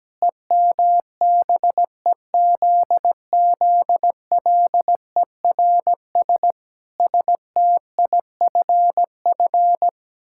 Morse code, heard as a keyed tone, about 17 wpm, 700 Hz. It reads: EMBEZZLERS STIFF
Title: Pangram Challenge v2 - 17wpm